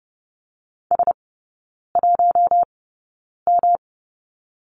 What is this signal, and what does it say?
Morse code, heard as a keyed tone, about 30 wpm, spaced as for 9 wpm, 700 Hz. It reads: S1M
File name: Lesson 18e - Letter m - Three Characters 30wpm